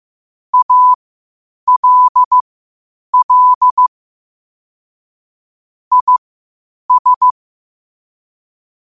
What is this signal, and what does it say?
Morse code, heard as a keyed tone, about 15 wpm, spaced as for 8 wpm, 1000 Hz. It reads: ALL IS
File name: qotdfiles / 8wpm